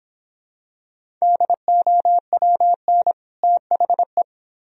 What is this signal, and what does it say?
Morse code, heard as a keyed tone, about 26 wpm, 700 Hz. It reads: DOWN THE